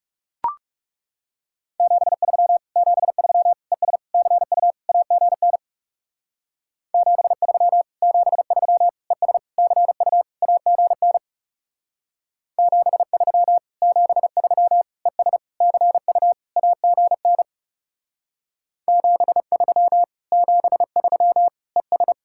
Morse code, heard as a keyed tone, about 37 wpm, 700 Hz. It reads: 73 73 ES CU AGN 73 73 ES CU AGN 73 73 ES CU AGN 73 73 ES
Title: Speed-Racing Common QSO Phrases - 30wpm